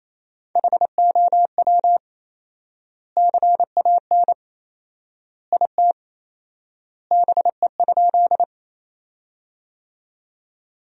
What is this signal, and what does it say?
Morse code, heard as a keyed tone, about 28 wpm, 700 Hz. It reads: HOW CAN IT BE?
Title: 4x Word-Spacing - Sentences from Top 100 Words - 28wpm